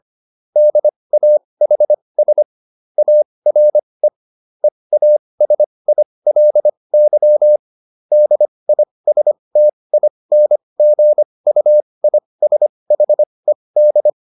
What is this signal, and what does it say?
Morse code, heard as a keyed tone, about 25 wpm, spaced as for 20 wpm, 600 Hz. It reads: DAHS ARE EASILY DISTINGUISHED